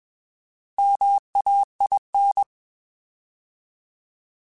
Morse code, heard as a keyed tone, about 21 wpm, 800 Hz. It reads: MAIN
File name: Alphabetical / Alphabetical mp3s / Ms @20